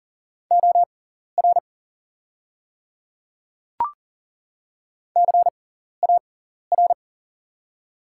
Morse code, heard as a keyed tone, about 40 wpm, spaced as for 14 wpm, 700 Hz. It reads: OR CAR
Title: 6x Character Spacing - Top 300 Words - 40wpm